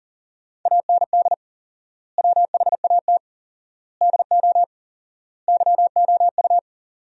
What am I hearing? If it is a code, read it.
Morse code, heard as a keyed tone, about 40 wpm, 700 Hz. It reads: AND WHAT DO YOU
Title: 4x Word-Spacing - Sentences from Top 100 Words - 40wpm